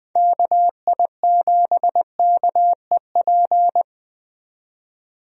Morse code, heard as a keyed tone, about 20 wpm, 700 Hz. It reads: KI7KEP
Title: USA Call Signs - 6 Characters - 20wpm